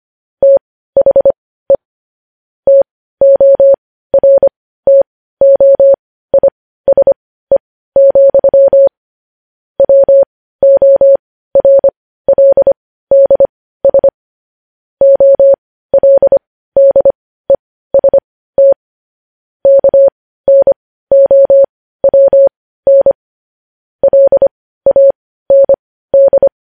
Morse code, heard as a keyed tone, about 25 wpm, spaced as for 15 wpm, 550 Hz. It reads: THE TORTOISE, WORLDS OLDEST KNOWN LAND
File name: podcast / News headlines-1x-25x15